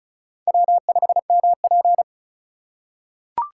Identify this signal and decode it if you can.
Morse code, heard as a keyed tone, about 34 wpm, 700 Hz. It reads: W5MP
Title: USA Call Signs - 4 Characters - 3x Character Spacing - 35wpm